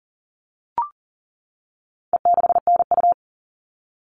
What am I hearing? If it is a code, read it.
Morse code, heard as a keyed tone, about 40 wpm, 700 Hz. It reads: E6NU
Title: All Cut Numbers - 4-Digit 40wpm